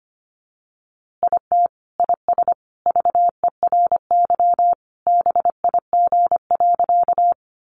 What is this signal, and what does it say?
Morse code, heard as a keyed tone, about 25 wpm, 700 Hz. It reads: IT IS VERY BIG.
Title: Easy Phrases 25wpm